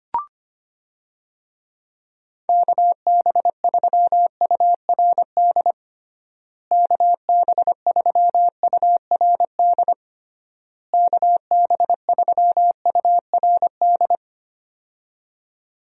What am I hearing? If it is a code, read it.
Morse code, heard as a keyed tone, about 25 wpm, 700 Hz. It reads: KB3URD KB3URD KB3URD
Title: USA Call Signs - 6 Characters - Repeated 3x - 25wpm